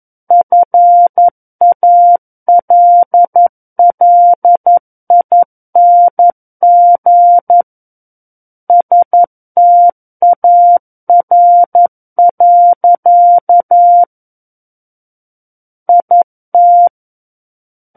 Morse code, heard as a keyed tone, about 11 wpm, 700 Hz. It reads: FALLING STAR. IT
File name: 11 WPM